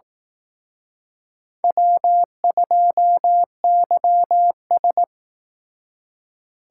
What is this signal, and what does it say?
Morse code, heard as a keyed tone, about 18 wpm, 700 Hz. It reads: W2YS
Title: Speed Racing - USA Call Signs 15wpm